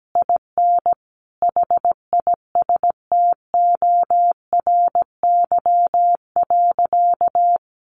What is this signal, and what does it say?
Morse code, heard as a keyed tone, about 17 wpm, 700 Hz. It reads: IN HISTORY.
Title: A Sound Of Thunder 17wpm